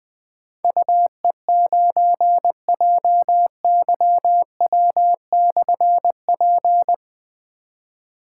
Morse code, heard as a keyed tone, about 20 wpm, 700 Hz. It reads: UE9JYW/P
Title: Difficult Call Signs 20wpm